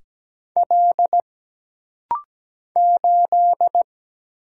Morse code, heard as a keyed tone, about 17 wpm, 700 Hz. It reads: L 8
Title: Single Letter-Number Alt - Rapid-Fire 17wpm